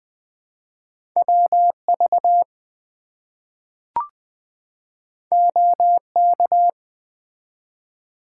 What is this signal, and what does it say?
Morse code, heard as a keyed tone, about 20 wpm, 700 Hz. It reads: WV OK
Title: US States and Canadian Provinces 20wpm